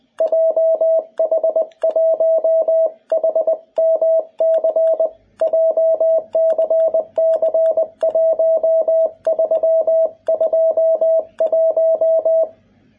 Morse code, heard as a keyed tone, about 20 wpm, 650 Hz, with radio noise. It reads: JH1HMC JCC1321
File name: jh8cbh / 2021cw